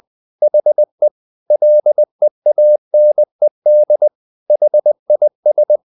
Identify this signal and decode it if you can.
Morse code, heard as a keyed tone, about 20 wpm, 600 Hz. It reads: HE LEANED HIS
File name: the old man and the sea part 2 20wpm